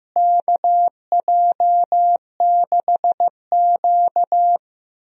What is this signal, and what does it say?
Morse code, heard as a keyed tone, about 15 wpm, 700 Hz. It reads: KJ6Q
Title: USA Call Signs - 4 Characters - Repeated 3x - 15wpm